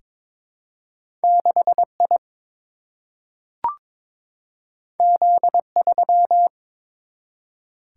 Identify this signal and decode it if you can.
Morse code, heard as a keyed tone, about 22 wpm, 700 Hz. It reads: 6 I Z 3
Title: Single-Letter Single-Number 22wpm